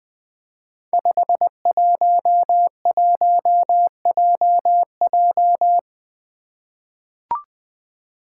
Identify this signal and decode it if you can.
Morse code, heard as a keyed tone, about 20 wpm, 700 Hz. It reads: 511JJ
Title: Five Challenging Letter-Number Combinations 20wpm